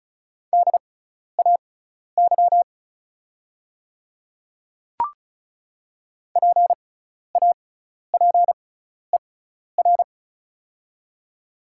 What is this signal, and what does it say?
Morse code, heard as a keyed tone, about 36 wpm, spaced as for 12 wpm, 690 Hz. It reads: DAY PAPER
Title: 6x Character Spacing - Top 300 Words - 35wpm